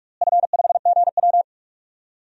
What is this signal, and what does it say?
Morse code, heard as a keyed tone, about 43 wpm, 700 Hz. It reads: F5GW
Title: DXCC-100 Call Signs - 4 Characters - Repeated 2x - 45wpm